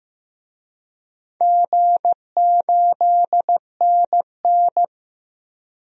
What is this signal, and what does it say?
Morse code, heard as a keyed tone, about 15 wpm, 700 Hz. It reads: G8NN
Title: DXCC-100 Call Signs - 4 Characters - 15wpm